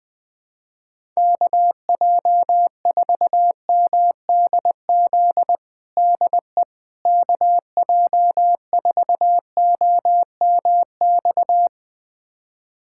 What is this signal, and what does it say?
Morse code, heard as a keyed tone, about 20 wpm, 700 Hz. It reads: KJ4MDZ DE KJ4OMX
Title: USA Callsign DE Callsign - 6 Characters - 20wpm